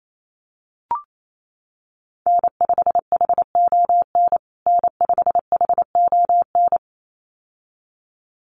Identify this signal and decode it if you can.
Morse code, heard as a keyed tone, about 28 wpm, 700 Hz. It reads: N5HON N5HON
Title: USA Call Signs Repeated 2x 28wpm